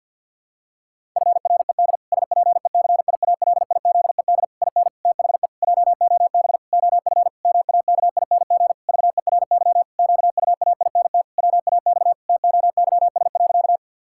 Morse code, heard as a keyed tone, about 52 wpm, 700 Hz. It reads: FRED SPECIALIZED IN THE JOB OF MAKING VERY QUAINT WAX TOYS.
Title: Pangram Challenge v2 - 50wpm